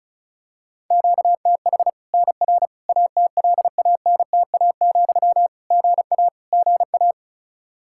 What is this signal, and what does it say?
Morse code, heard as a keyed tone, about 36 wpm, 700 Hz. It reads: QTH NR ATLANTA, GA GA
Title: Practice QSO - Exchange 1 - Beginning - 35wpm